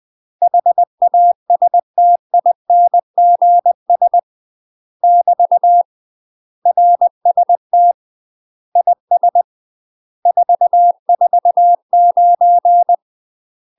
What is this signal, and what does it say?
Morse code, heard as a keyed tone, about 20 wpm, 700 Hz. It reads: HASTINGS <BT> RST IS 449
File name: UK MASTER QSOs TRAINING/20WPM